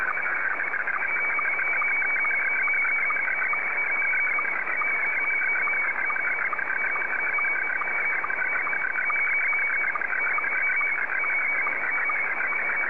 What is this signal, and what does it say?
SSTV (slow-scan television), Scottie 1.